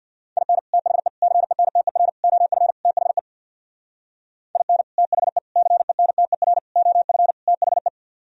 Morse code, heard as a keyed tone, about 64 wpm, 700 Hz. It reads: IN THE CENTER OF THE IN THE CENTER OF THE